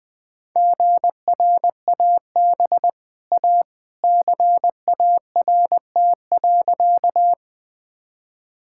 Morse code, heard as a keyed tone, about 20 wpm, 700 Hz. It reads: GRAB A CART.
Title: Easy Sentences with Hints - 20wpm